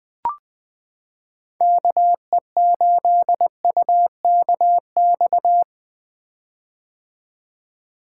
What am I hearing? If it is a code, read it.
Morse code, heard as a keyed tone, about 20 wpm, 700 Hz. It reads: KE8UKX
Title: 2023 POTA Hunter Call Signs - 20wpm